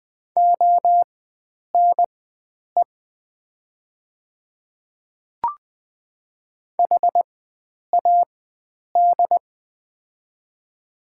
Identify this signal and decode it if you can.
Morse code, heard as a keyed tone, about 20 wpm, spaced as for 9 wpm, 700 Hz. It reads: ONE HAD
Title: 4x Character Spacing - Top 100 Words - 20wpm